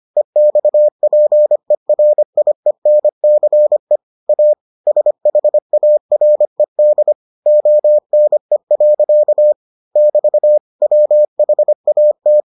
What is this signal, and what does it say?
Morse code, heard as a keyed tone, about 25 wpm, 600 Hz. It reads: EXPERIENCE A SHARED ONE. <BT> WHAT